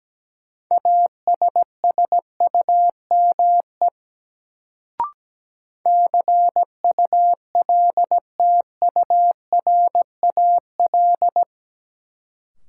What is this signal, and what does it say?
Morse code, heard as a keyed tone, about 17 wpm, 700 Hz. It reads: ASSUME CULTURAL
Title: Sentences from Top 1500 Words - New Word Drill - 17wpm